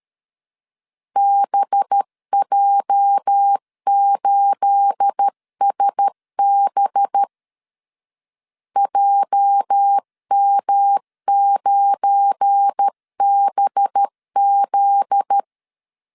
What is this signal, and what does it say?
Morse code, heard as a keyed tone, about 13 wpm, 800 Hz. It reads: BJ8SB JM9BZ
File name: LEZIONI / 9 28 12wpm 2351